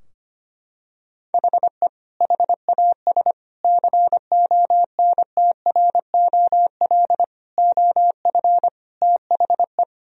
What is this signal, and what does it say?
Morse code, heard as a keyed tone, about 25 wpm, 700 Hz. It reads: HE HAS CONTROL OF THE
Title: Sentences from Top 1000 Words - Review Part 3 - 25wpm